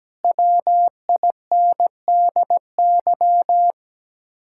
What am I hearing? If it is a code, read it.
Morse code, heard as a keyed tone, about 17 wpm, 700 Hz. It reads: WINDY
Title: Speed-Racing Common QSO Phrases - 17wpm